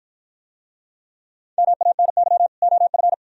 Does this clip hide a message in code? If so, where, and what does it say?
Yes: Morse code from 1.6 to 3.1 s, heard as a keyed tone, about 51 wpm, 700 Hz. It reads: MANY OF